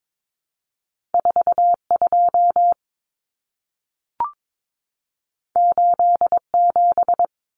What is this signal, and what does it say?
Morse code, heard as a keyed tone, about 22 wpm, 700 Hz. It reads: 42 87
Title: Common Cut Numbers - 2-Digit 22wpm